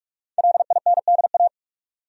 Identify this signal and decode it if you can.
Morse code, heard as a keyed tone, about 46 wpm, 700 Hz. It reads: LINDA
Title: USA Common Names - Max 5 Letters - 45wpm